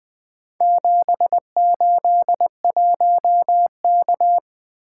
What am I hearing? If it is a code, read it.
Morse code, heard as a keyed tone, about 20 wpm, 700 Hz. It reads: 781K
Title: Four Letter-Number Combinations - 20wpm